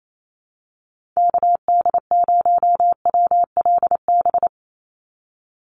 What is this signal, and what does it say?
Morse code, heard as a keyed tone, about 28 wpm, 700 Hz. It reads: KD0WLB
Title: USA Call Signs 28wpm